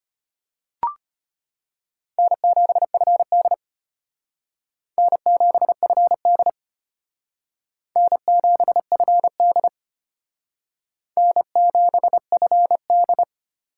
Morse code, heard as a keyed tone, about 31 wpm, 700 Hz. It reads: N7FD N7FD N7FD N7FD
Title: Speed Racing - USA 1x2 Call Signs 25wpm